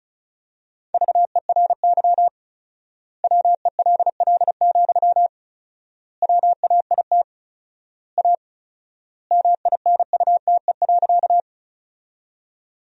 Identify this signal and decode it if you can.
Morse code, heard as a keyed tone, about 36 wpm, 700 Hz. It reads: VERY WELL, WAIT A MINUTE.
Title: 4x Word-Spacing - Sentences from Top 500 Words - 35wpm